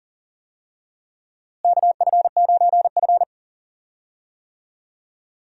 Morse code, heard as a keyed tone, about 37 wpm, 700 Hz. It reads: KF9F
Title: USA Call Signs - 4 Characters - Repeated 2x - 40wpm